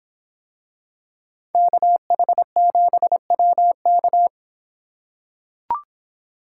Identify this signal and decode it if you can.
Morse code, heard as a keyed tone, about 25 wpm, 700 Hz. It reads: KH7WK